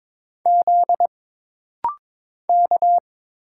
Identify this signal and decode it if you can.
Morse code, heard as a keyed tone, about 22 wpm, 700 Hz. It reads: Z K
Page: This rapid-fire timing is geared to push your ICR (Instant Character Recognition) skills.